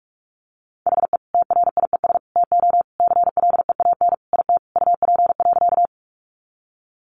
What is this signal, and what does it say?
Morse code, heard as a keyed tone, about 44 wpm, 700 Hz. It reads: HE TRIES TO CLEAN IT UP.
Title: Additional Easy Phrases 45wpm